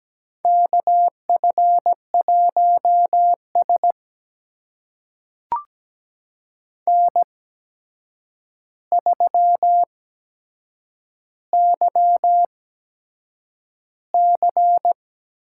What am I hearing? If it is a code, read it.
Morse code, heard as a keyed tone, about 17 wpm, spaced as for 7 wpm, 700 Hz. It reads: KF1S N3YC
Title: USA Call Signs - 4 Characters - 8x Character Spacing - 17wpm